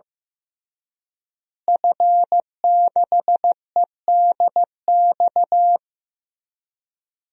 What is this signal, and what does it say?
Morse code, heard as a keyed tone, about 15 wpm, 700 Hz. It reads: F6EDX
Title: DXCC-100 Call Signs - 5 Characters - Repeated 3x - 15wpm